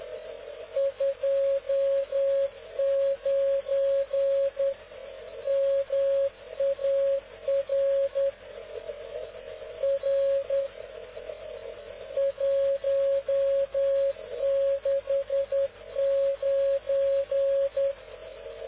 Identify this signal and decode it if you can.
Morse code, heard as a keyed tone, about 10 wpm, 550 Hz. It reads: 29 MAR R 169
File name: cw / ubacw dk0wcy slow